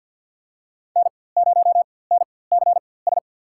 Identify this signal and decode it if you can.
Morse code, heard as a keyed tone, about 47 wpm, spaced as for 23 wpm, 700 Hz. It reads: N0NCS
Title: USA Call Signs - 5 Characters - 4x Character Spacing - 50wpm